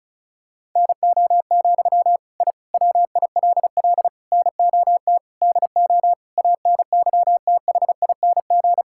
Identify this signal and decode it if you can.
Morse code, heard as a keyed tone, about 36 wpm, 700 Hz. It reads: NO, I WILL NOT DO ANYTHING
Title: Sentences from Top 1000 Words - Review Part 6 - 35wpm